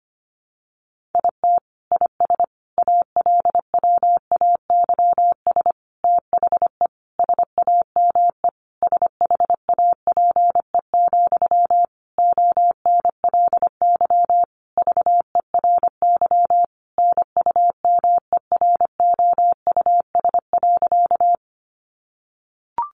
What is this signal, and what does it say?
Morse code, heard as a keyed tone, about 25 wpm, 700 Hz. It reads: IT IS ALWAYS THE SAME SHAPE, ONLY VERY NUMEROUS.
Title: Sentences from Top 1400 Words 25wpm